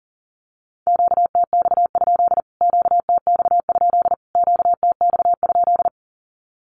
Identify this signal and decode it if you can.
Morse code, heard as a keyed tone, about 40 wpm, 700 Hz. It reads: QTX? QTX? QTX?
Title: Q-Codes Repeated 3x 40wpm